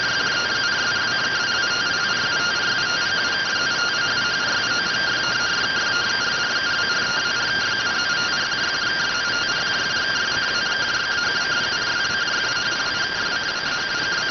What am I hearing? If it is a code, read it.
Two-tone teleprinter signal, about 170 Hz shift.